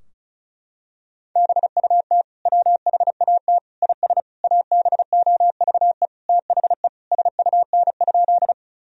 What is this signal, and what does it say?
Morse code, heard as a keyed tone, about 36 wpm, 700 Hz. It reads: BUT WHAT IS ABOVE THE SUN?
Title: Sentences from Top 1000 Words - Review Part 2 - 35wpm